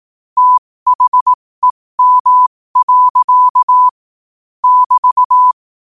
Morse code, heard as a keyed tone, about 18 wpm, spaced as for 15 wpm, 1000 Hz. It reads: THEM. <BT>